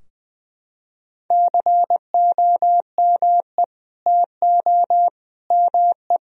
Morse code, heard as a keyed tone, about 20 wpm, 700 Hz. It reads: COME TO ME
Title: Sentences from Top 1000 Words - Review Part 4 - 20wpm